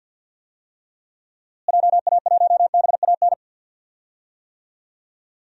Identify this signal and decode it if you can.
Morse code, heard as a keyed tone, about 46 wpm, 700 Hz. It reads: JA1BAN
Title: DXCC-100 Call Signs - 6 Characters - Repeated 2x - 50wpm